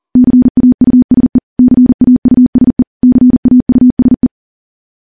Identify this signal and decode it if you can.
Morse code, heard as a keyed tone, about 40 wpm, 250 Hz. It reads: CAUSE CAUSE CAUSE
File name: morse / morse-training-40WPM